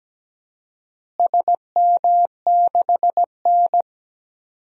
Morse code, heard as a keyed tone, about 17 wpm, 700 Hz. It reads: SM6N
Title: DXCC-100 Call Signs - 4 Characters - Repeated 3x - 17wpm